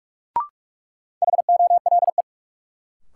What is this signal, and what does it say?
Morse code, heard as a keyed tone, about 42 wpm, 700 Hz. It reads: HOLE
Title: Sentences from Top 1600 Words - New Word Drill Alt - 45wpm